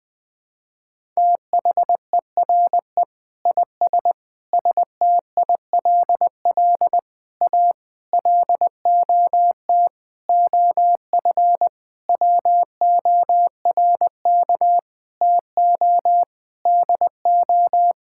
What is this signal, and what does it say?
Morse code, heard as a keyed tone, about 21 wpm, 700 Hz. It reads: THERE IS STILL A LOT OF WORK TO DO
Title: Sentences with 9-Word Phrases - 20wpm